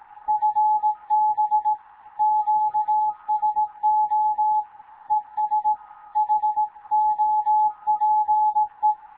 Morse code, heard as a keyed tone, about 18 wpm, 850 Hz. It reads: FB QSO ES HOPE
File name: Morze-HA2MN / hangp / 2 11 GYAKORLAT